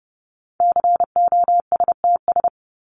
Morse code, heard as a keyed tone, about 30 wpm, 700 Hz. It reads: COSTS
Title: ES:S Suffix - Top 100 Words with Base Words 30wpm